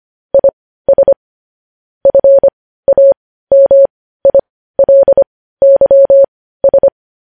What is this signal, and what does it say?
Morse code, heard as a keyed tone, about 25 wpm, spaced as for 15 wpm, 550 Hz. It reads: IS FAMILYS
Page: news-headlines-in-morse-code-at-15-wpm